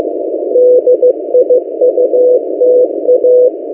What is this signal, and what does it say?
Morse code, heard as a keyed tone, about 15 wpm, 500 Hz. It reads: DIUTA
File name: M08a- -1355utc-8096khz